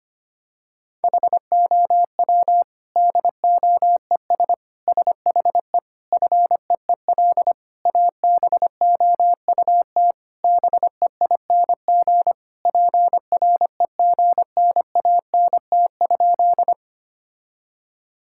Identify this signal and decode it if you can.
Morse code, heard as a keyed tone, about 25 wpm, 700 Hz. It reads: HOW DOES SHE FEEL ABOUT BEING PREGNANT?
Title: Sentences from Top 1000 Words - Review Part 6 - 25wpm